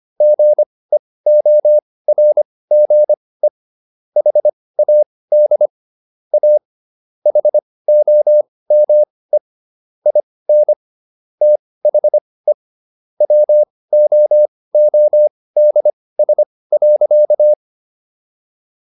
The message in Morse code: GEORGE HAD A HOME IN THE WOODS.

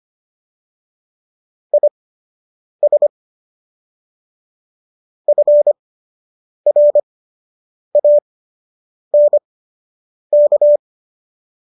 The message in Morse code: IS FRANK

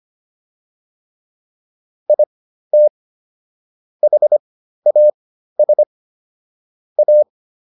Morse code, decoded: IT HAS A

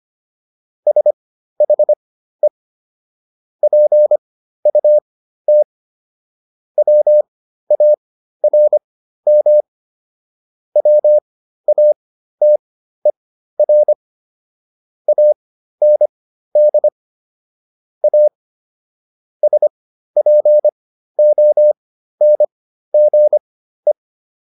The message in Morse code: SHE PUT WARM WATER AND A SPONGE

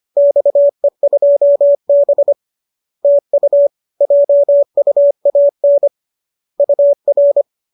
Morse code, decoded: XE2B TU JUAN UR

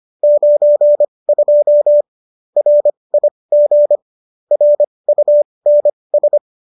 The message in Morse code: 92 RIG RUNS

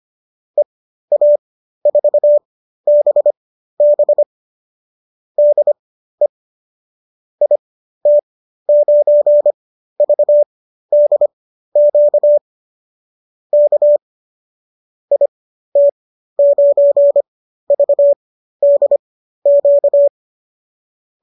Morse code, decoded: EA4BB DE IT9VDQ K IT9VDQ

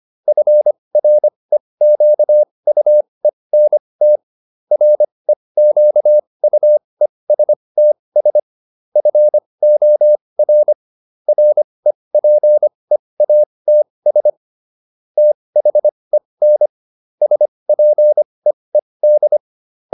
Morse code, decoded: FREQUENT REQUESTS FOR REPEATS THEN SPEED

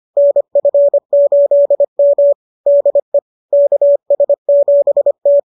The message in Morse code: NF8M DE KS7T

